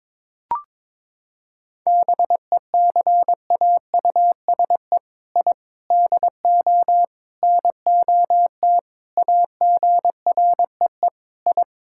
Morse code, heard as a keyed tone, about 21 wpm, 700 Hz. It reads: BECAUSE I DO NOT AGREE I
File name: practice / 22wpm / Sentences with 4-Word Phrases - 22wpm